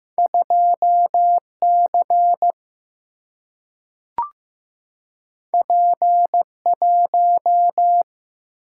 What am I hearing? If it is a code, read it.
Morse code, heard as a keyed tone, about 15 wpm, 700 Hz. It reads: 2 C P 1
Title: Single-Letter Single-Number 15wpm